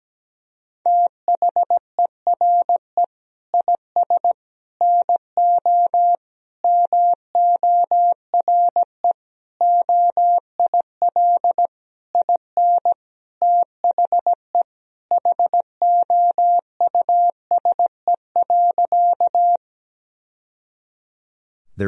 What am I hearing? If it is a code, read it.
Morse code, heard as a keyed tone, about 17 wpm, 700 Hz. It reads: THERE IS NO MORE OIL IN THE HOUSE.
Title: Sentences from Top 1000 Words - Review Part 4 - 17wpm